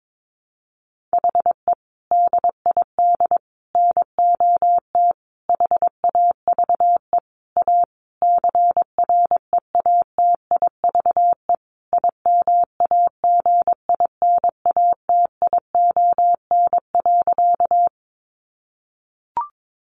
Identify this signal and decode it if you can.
Morse code, heard as a keyed tone, about 22 wpm, 700 Hz. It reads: HE DID NOT HAVE A CREATIVE IMAGINATION.
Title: Sentences from Top 1300 Words 22wpm